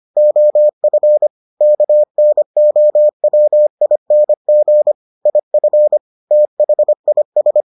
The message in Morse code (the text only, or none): OF KNOWING IF THIS